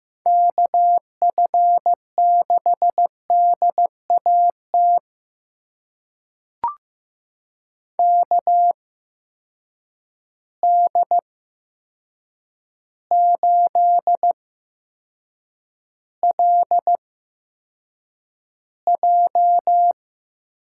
Morse code, heard as a keyed tone, about 15 wpm, spaced as for 7 wpm, 700 Hz. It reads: KF6DAT KD8LJ
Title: USA Call Signs - 6 Characters - 8x Character Spacing - 15wpm